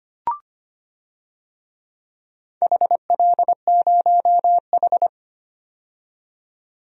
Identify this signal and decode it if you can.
Morse code, heard as a keyed tone, about 25 wpm, 700 Hz. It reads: HL0H
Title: DXCC-100 Call Signs - 4 Characters - 25wpm